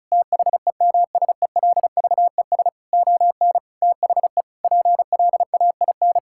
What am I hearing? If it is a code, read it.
Morse code, heard as a keyed tone, about 36 wpm, 700 Hz. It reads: THEMSELVES ON THE PLAIN
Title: Sentences from Top 1000 Words - Review Part 5 - 35wpm